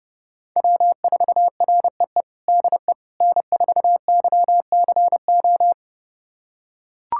Morse code, heard as a keyed tone, about 30 wpm, 700 Hz. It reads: W4REE DE N4YCO
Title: USA Callsign DE Callsign - 5 Characters - 30wpm